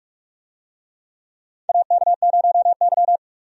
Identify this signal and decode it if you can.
Morse code, heard as a keyed tone, about 46 wpm, 700 Hz. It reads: AK0Y
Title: USA Call Signs - 4 Characters - 45wpm